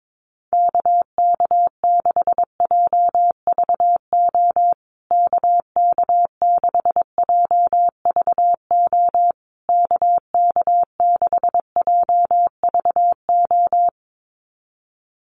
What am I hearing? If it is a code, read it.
Morse code, heard as a keyed tone, about 22 wpm, 690 Hz. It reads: KK6JVO KK6JVO KK6JVO
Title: USA Call Signs Repeated 3x 22wpm